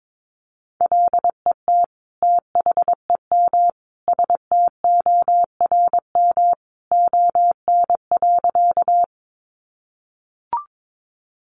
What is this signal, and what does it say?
Morse code, heard as a keyed tone, about 22 wpm, 700 Hz. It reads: LET THEM STORM ON.
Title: Sentences from Top 1500 Words 22wpm